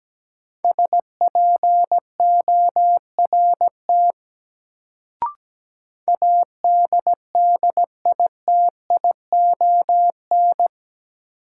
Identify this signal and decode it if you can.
Morse code, heard as a keyed tone, about 17 wpm, 700 Hz. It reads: SPORT ADDITION
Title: Sentences from Top 1500 Words - New Word Drill Alt - 17wpm